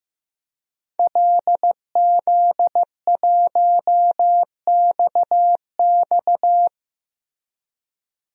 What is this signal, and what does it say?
Morse code, heard as a keyed tone, about 15 wpm, 700 Hz. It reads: LZ1XX
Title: DXCC-100 Call Signs - 5 Characters - Repeated 2x - 15wpm